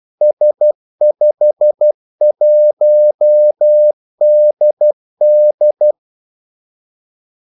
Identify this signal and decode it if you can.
Morse code, heard as a keyed tone, about 12 wpm, 600 Hz. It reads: S51DD
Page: 2 ekstra ordmellemrum mellem kaldesignalerne